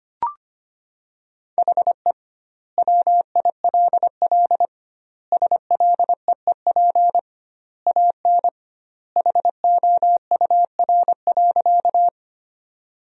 Morse code, heard as a keyed tone, about 25 wpm, 700 Hz. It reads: HE WILL SLEEP AN HOUR.
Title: 2x Word-Spacing - Sentences from Top 500 Words - 25wpm